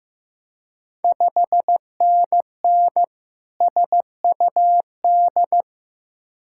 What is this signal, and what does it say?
Morse code, heard as a keyed tone, about 15 wpm, 700 Hz. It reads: 5NN SUD
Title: Ontario QSO Party Exchange 15wpm